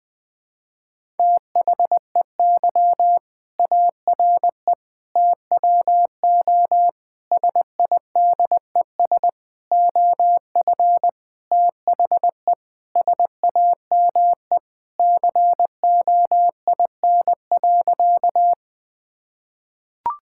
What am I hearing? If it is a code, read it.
Morse code, heard as a keyed tone, about 20 wpm, 700 Hz. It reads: THEY ARE TWO SIDES OF THE SAME COIN.
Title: Sentences with 7-Word Phrases - 20wpm